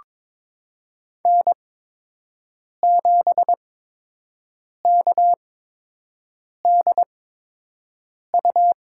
Morse code, heard as a keyed tone, about 22 wpm, spaced as for 6 wpm, 700 Hz. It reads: N7KDU
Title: USA Call Signs - 5 Characters - 8x Character Spacing - 22wpm